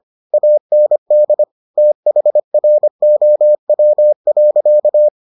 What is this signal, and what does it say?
Morse code, heard as a keyed tone, about 25 wpm, 600 Hz. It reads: AND THROW.